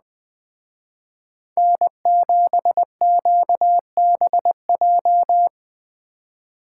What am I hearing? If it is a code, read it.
Morse code, heard as a keyed tone, about 19 wpm, 700 Hz. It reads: N7QBJ
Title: Speed Racing - USA Call Signs 17wpm